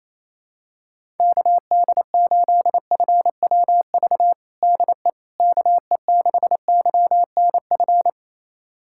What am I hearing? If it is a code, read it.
Morse code, heard as a keyed tone, about 28 wpm, 700 Hz. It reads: KD8FWV DE KE6YNF
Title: USA Callsign DE Callsign 28wpm